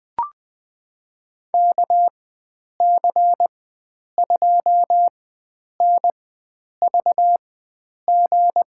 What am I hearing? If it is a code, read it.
Morse code, heard as a keyed tone, about 20 wpm, spaced as for 9 wpm, 700 Hz. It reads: KC2NVG
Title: USA Call Signs - 6 Characters - 4x Character Spacing - 20wpm